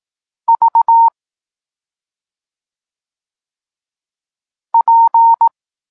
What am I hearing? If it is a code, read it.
Morse code, heard as a keyed tone, about 18 wpm, 920 Hz. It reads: V P
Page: Tekst 32 – Doskonalenie odbioru poznanych liter